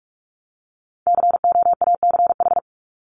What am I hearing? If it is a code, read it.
Morse code, heard as a keyed tone, about 44 wpm, 700 Hz. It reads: COACH
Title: Top 1751-2000 Words 45wpm